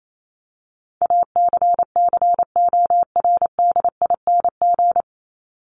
Morse code, heard as a keyed tone, about 28 wpm, 700 Hz. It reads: ACCORDING
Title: LY Suffix - Top 100 Words with Base Words 28wpm